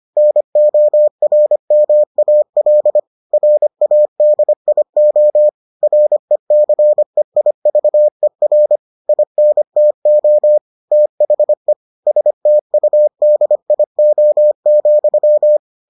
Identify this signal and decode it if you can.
Morse code, heard as a keyed tone, about 25 wpm, 600 Hz. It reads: NORMAL RADIO RECEIVER INTO THE STUDIO,